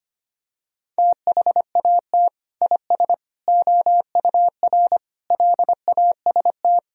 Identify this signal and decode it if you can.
Morse code, heard as a keyed tone, about 25 wpm, 700 Hz. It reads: THAT IS OUR LAST
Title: Sentences from Top 1000 Words - Review Part 4 - 25wpm